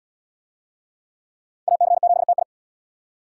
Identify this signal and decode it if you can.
Morse code, heard as a keyed tone, about 74 wpm, 700 Hz. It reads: AC7TI